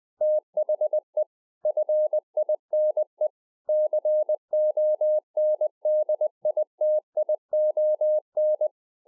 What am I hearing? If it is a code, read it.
Morse code, heard as a keyed tone, about 20 wpm, 610 Hz. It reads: THE FINE CONDITION
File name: 20WPM